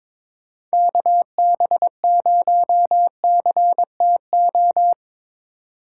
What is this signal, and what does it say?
Morse code, heard as a keyed tone, about 22 wpm, 700 Hz. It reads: KB0CTO
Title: USA Call Signs - 6 Characters - 3x Character Spacing - 22wpm